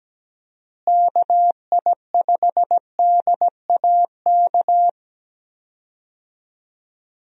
Morse code, heard as a keyed tone, about 17 wpm, 700 Hz. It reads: KI5DAK
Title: USA Call Signs - 6 Characters - Repeated 3x - 17wpm